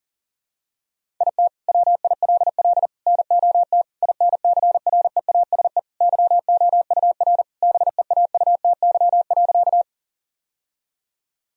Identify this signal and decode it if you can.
Morse code, heard as a keyed tone, about 40 wpm, 700 Hz. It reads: IT WILL NOT INCREASE YOUR BEAUTY.
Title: Sentences from Top 1000 Words - Review Part 4 - 40wpm